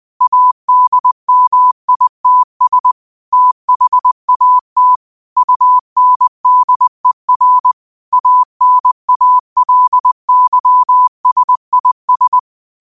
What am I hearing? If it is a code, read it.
Morse code, heard as a keyed tone, about 20 wpm, 1000 Hz. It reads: ADMITS THAT UNDER ANALYSIS
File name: qotdfiles / 20wpm